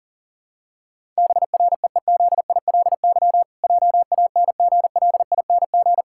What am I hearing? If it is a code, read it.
Morse code, heard as a keyed tone, about 40 wpm, 700 Hz. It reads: BREEZILY JANGLING
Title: Pangram Challenge v2 - 40wpm